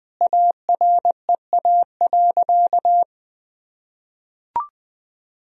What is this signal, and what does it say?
Morse code, heard as a keyed tone, about 21 wpm, 700 Hz. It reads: AREA.
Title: Sentences from Top 1000 Words - Review Part 2 - 20wpm